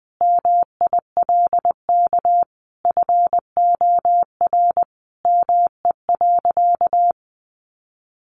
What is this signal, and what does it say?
Morse code, heard as a keyed tone, about 20 wpm, 700 Hz. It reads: MILK FOR ME.
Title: Sentences from Top 1500 Words 20wpm